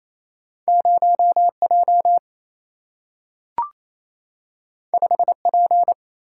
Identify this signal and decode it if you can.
Morse code, heard as a keyed tone, about 28 wpm, 700 Hz. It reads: 0J 5P
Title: Single-Letter Single-Number 28wpm